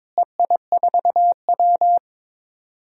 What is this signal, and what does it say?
Morse code, heard as a keyed tone, about 23 wpm, 700 Hz. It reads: EI4W
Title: DXCC-100 Call Signs - 4 Characters - Repeated 3x - 22wpm